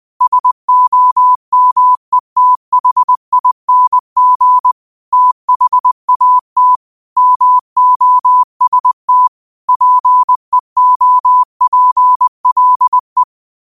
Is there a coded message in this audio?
Yes: Morse code from 0.2 to 13.2 s, heard as a keyed tone, about 20 wpm, 1000 Hz. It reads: SOMETHING THAT MOST PEOPLE